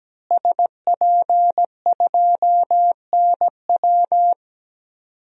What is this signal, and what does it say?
Morse code, heard as a keyed tone, about 17 wpm, 700 Hz. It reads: SP2NW
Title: DXCC-100 Call Signs - 5 Characters - Repeated 2x - 17wpm